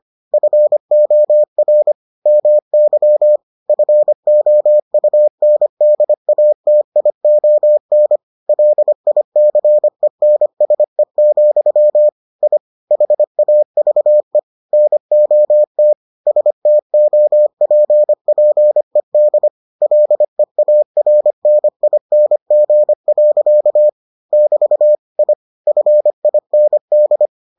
Morse code, heard as a keyed tone, about 25 wpm, 600 Hz. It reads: FOR MY FOUNDATION LICENSE, I HAVE NOT STOPPED LEARNING. <BT> I FIND